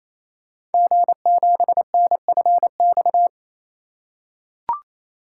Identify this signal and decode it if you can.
Morse code, heard as a keyed tone, about 27 wpm, 700 Hz. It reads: G7NFX
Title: Code Groups - 5 Letter:Numbers - 28wpm